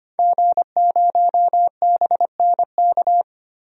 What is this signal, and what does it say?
Morse code, heard as a keyed tone, about 25 wpm, 700 Hz. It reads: G0BNK
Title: DXCC-100 Call Signs - 5 Characters - Repeated 2x - 25wpm